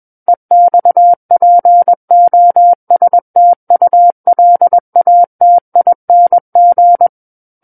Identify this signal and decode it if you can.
Morse code, heard as a keyed tone, about 21 wpm, 700 Hz. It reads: EXPOSTULATING
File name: Book One / 12-Chapter 12 - 21 WPM